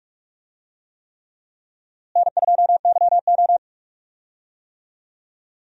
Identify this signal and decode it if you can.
Morse code, heard as a keyed tone, about 46 wpm, 700 Hz. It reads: N2YO
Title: USA Call Signs - 4 Characters - Repeated 3x - 45wpm